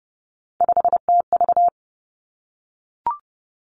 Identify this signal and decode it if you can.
Morse code, heard as a keyed tone, about 30 wpm, 700 Hz. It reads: 5TV
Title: Three Challenging Letter-Number Combinations 30wpm